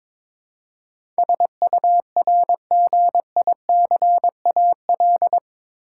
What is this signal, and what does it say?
Morse code, heard as a keyed tone, about 22 wpm, 700 Hz. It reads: SURGICAL
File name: practice / 22wpm / Eight Letter Words - 22wpm